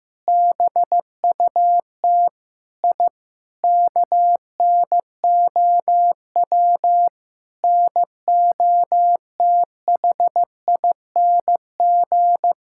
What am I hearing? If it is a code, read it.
Morse code, heard as a keyed tone, about 15 wpm, 700 Hz. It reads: BUT I KNOW NOTHING
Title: Sentences from Top 1000 Words - Review Part 6 - 15wpm